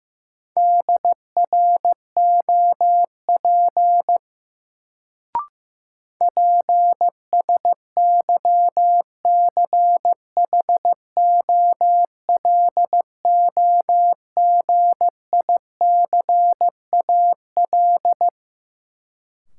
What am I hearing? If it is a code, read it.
Morse code, heard as a keyed tone, about 15 wpm, 700 Hz. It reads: DROP PSYCHOLOGICAL
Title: Sentences from Top 1400 Words - New Word Drill - 15wpm